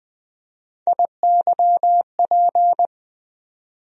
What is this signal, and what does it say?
Morse code, heard as a keyed tone, about 20 wpm, 700 Hz. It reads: IYP
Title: USA Call Sign Postfixes - 3 Letters 20wpm